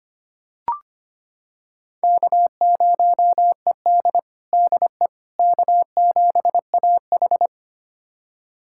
Morse code, heard as a keyed tone, about 25 wpm, 700 Hz. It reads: K0ED DE K7AH
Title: USA Callsign DE Callsign - 4 Characters - 25wpm